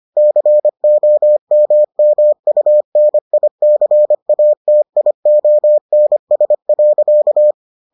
Morse code, heard as a keyed tone, about 25 wpm, 600 Hz. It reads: COMMUNICATIONS.